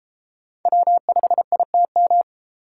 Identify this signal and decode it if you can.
Morse code, heard as a keyed tone, about 31 wpm, 700 Hz. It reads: W5ITM